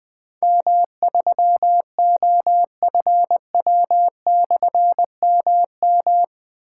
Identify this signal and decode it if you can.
Morse code, heard as a keyed tone, about 21 wpm, 700 Hz. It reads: M3OFW/MM
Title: Difficult Call Signs 20wpm